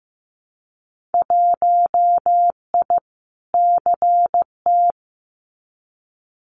Morse code, heard as a keyed tone, about 15 wpm, 700 Hz. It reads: 1I CT
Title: Winter Field Day Exchanges 15wpm